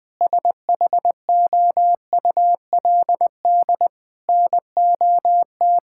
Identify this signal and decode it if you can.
Morse code, heard as a keyed tone, about 19 wpm, 700 Hz. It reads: SHOULD NOT
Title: Sentences from Top 1000 Words - Review Part 5 - 20wpm